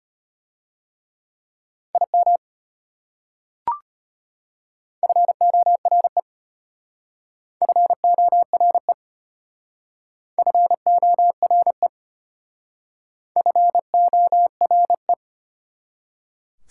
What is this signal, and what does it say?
Morse code, heard as a keyed tone, about 33 wpm, 700 Hz. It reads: IM FORE FORE FORE FORE